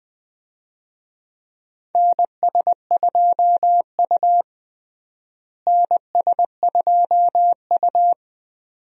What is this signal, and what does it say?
Morse code, heard as a keyed tone, about 20 wpm, 700 Hz. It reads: NS2U NS2U
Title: USA Call Signs - 4 Characters - Repeated 2x - 20wpm